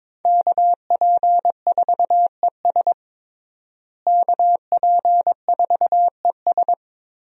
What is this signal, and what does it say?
Morse code, heard as a keyed tone, about 22 wpm, 700 Hz. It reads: KP4ES KP4ES
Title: USA Call Signs - 5 Characters - Repeated 3x - 22wpm